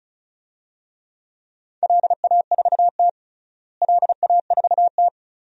Morse code, heard as a keyed tone, about 36 wpm, 700 Hz. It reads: LA4T LA4T
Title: DXCC-100 Call Signs - 4 Characters - Repeated 3x - 35wpm